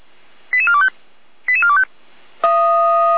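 Five-tone selective call (ZVEI), heard as quick runs of five tones, tones of about 70 ms. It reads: 88425 88425
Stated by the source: ZVEI five-tone call received by SDR for feuerwehr-orlamuende, a fire brigade at Orlamünde.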